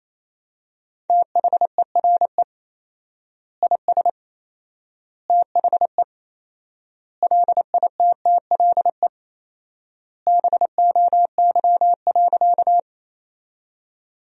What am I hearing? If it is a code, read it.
Morse code, heard as a keyed tone, about 28 wpm, 700 Hz. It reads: THERE IS THE LITTLE BOY.
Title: 4x Word-Spacing - Sentences from Top 200 Words - 28wpm